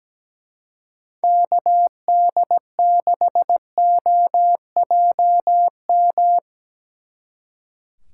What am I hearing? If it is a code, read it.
Morse code, heard as a keyed tone, about 19 wpm, 700 Hz. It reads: KD6OJM
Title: Speed Racing - USA Call Signs 17wpm